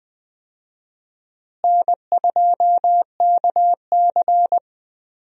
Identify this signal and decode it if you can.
Morse code, heard as a keyed tone, about 20 wpm, 700 Hz. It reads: N2KC
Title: USA Call Signs - 4 Characters - Repeated 3x - 20wpm